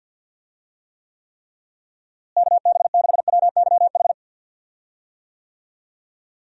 Morse code, heard as a keyed tone, about 52 wpm, 700 Hz. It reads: KB6WYH
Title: USA Call Signs - 6 Characters - Repeated 3x - 50wpm